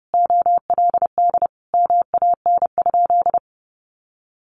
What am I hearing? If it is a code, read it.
Morse code, heard as a keyed tone, about 30 wpm, 700 Hz. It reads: OLD MAN?
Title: Sentences from Top 1100 Words 30wpm